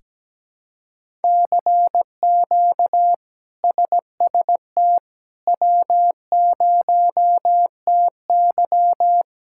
Morse code, heard as a keyed tone, about 17 wpm, 700 Hz. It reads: CQ SST W0TY
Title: K1USN SST QSO Practice - 17wpm